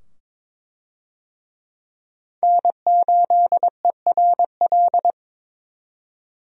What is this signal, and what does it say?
Morse code, heard as a keyed tone, about 22 wpm, 700 Hz. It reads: N8ERL
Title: USA Call Signs - 5 Characters - 22wpm